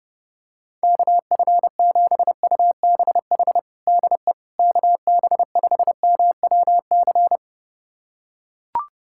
Morse code, heard as a keyed tone, about 30 wpm, 700 Hz. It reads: KF7UBH DE KB5MWC
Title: USA Callsign DE Callsign - 6 Characters - 30wpm